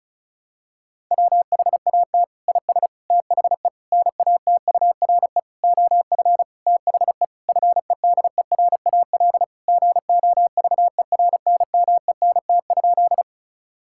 Morse code, heard as a keyed tone, about 36 wpm, 700 Hz. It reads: WHAT IS THE NATURE OF THE FEDERAL GOVERNMENT?
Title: Sentences from Top 1000 Words - Review Part 5 - 35wpm